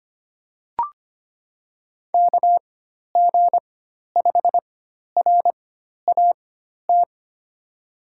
Morse code, heard as a keyed tone, about 25 wpm, spaced as for 12 wpm, 700 Hz. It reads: KG5RAT
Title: USA Call Signs - 6 Characters - 4x Character Spacing - 25wpm